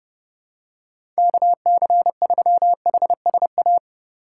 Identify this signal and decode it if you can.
Morse code, heard as a keyed tone, about 30 wpm, 700 Hz. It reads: KC3HSA
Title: USA Call Signs - 6 Characters - 4x Character Spacing - 30wpm